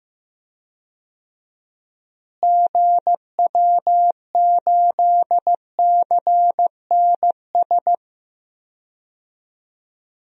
Morse code, heard as a keyed tone, about 15 wpm, 700 Hz. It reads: GW8CNS
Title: DXCC-100 Call Signs - 6 Characters - Repeated 2x - 15wpm